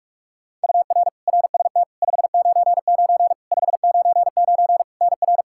Morse code, heard as a keyed tone, about 41 wpm, 700 Hz. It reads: UR RST 599 599 NR
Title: Practice QSO - Exchange 1 - Beginning - 45wpm